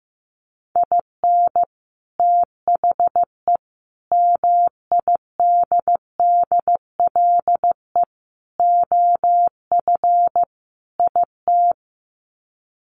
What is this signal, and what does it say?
Morse code, heard as a keyed tone, about 15 wpm, 700 Hz. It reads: IN THE MIDDLE OF IT
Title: Sets of 5 Words Encore 15wpm